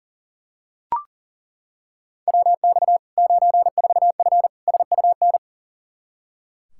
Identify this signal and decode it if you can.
Morse code, heard as a keyed tone, about 37 wpm, 700 Hz. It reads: WX 94F SUN
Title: Practice QSO - Exchange 2 - Info - 40wpm